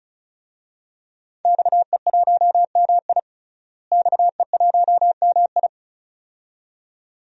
Morse code, heard as a keyed tone, about 36 wpm, 700 Hz. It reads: XE1MI XE1MI
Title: DXCC-100 Call Signs - 5 Characters - Repeated 2x - 35wpm